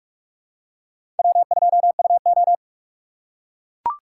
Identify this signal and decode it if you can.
Morse code, heard as a keyed tone, about 43 wpm, 700 Hz. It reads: W2UO
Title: USA Call Signs - 4 Characters - 4x Character Spacing - 45wpm